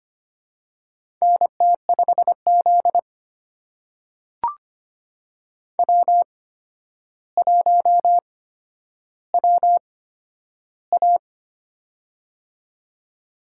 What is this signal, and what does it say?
Morse code, heard as a keyed tone, about 25 wpm, spaced as for 11 wpm, 700 Hz. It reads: NT5Z W1WA
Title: USA Call Signs - 4 Characters - 8x Character Spacing - 25wpm